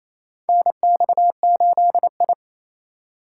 Morse code, heard as a keyed tone, about 28 wpm, 700 Hz. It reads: NX8I
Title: USA Call Signs - 4 Characters - 4x Character Spacing - 28wpm